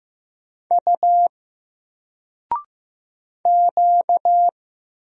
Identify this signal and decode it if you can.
Morse code, heard as a keyed tone, about 15 wpm, 700 Hz. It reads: U Q
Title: Single Letter-Number Alt 15wpm